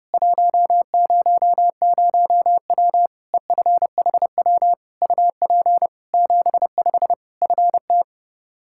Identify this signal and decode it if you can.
Morse code, heard as a keyed tone, about 30 wpm, 700 Hz. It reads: 100W EFHW UP 75 FT
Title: Practice QSO - Exchange 2 - Info - 30wpm